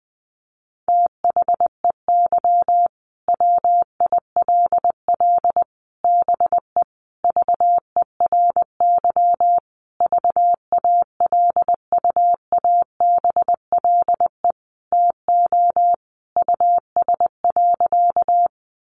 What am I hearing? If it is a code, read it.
Morse code, heard as a keyed tone, about 20 wpm, 700 Hz. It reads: THEY WILL BE VERY VALUABLE TO US.
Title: Sentences from Top 1100 Words 20wpm